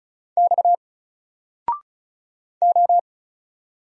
Morse code, heard as a keyed tone, about 36 wpm, 700 Hz. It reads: X O
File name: Single-Letter-Number-35wpm